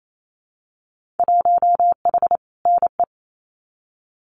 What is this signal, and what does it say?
Morse code, heard as a keyed tone, about 28 wpm, 700 Hz. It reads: 1H NE
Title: Winter Field Day Exchanges 28wpm